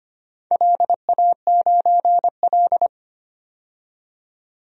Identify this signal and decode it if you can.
Morse code, heard as a keyed tone, about 25 wpm, 700 Hz. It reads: LA9L
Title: DXCC-100 Call Signs - 4 Characters - 25wpm